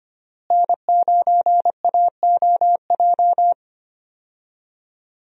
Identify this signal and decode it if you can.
Morse code, heard as a keyed tone, about 25 wpm, 700 Hz. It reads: N9AOJ
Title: USA Call Signs - 5 Characters - Repeated 3x - 25wpm